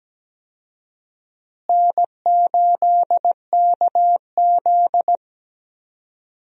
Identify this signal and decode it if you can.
Morse code, heard as a keyed tone, about 17 wpm, 700 Hz. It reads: N8KZ
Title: USA Call Signs - 4 Characters - Repeated 2x - 17wpm